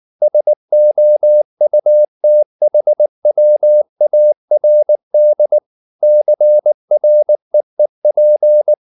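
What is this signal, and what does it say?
Morse code, heard as a keyed tone, about 19 wpm, 600 Hz. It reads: SOUTHWARD CREEP
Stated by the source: Morse code at 19 wpm from CW med Gnister, reading Ken Follett's novel Never in English.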